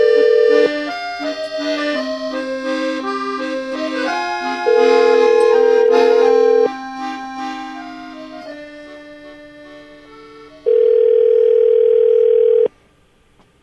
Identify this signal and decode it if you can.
Ringback tone (North America), 440 + 480 Hz, with noise.